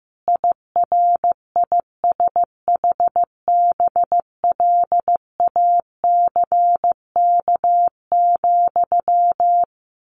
Morse code, heard as a keyed tone, about 15 wpm, 700 Hz. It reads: IRISHBLACK,
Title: A Sound Of Thunder 15wpm